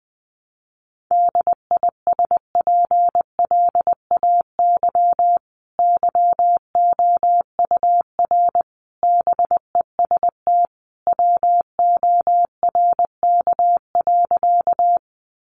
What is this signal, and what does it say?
Morse code, heard as a keyed tone, about 20 wpm, 700 Hz. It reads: DISPLAY YOUR BEST WORK.
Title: Sentences from Top 1200 Words 20wpm